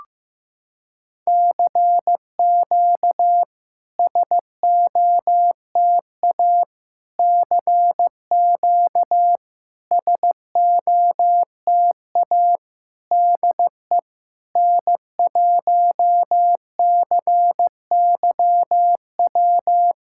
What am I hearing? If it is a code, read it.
Morse code, heard as a keyed tone, about 15 wpm, 700 Hz. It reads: CQ SOTA CQ SOTA DE N1CYW
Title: SOTA QSO Practice 15wpm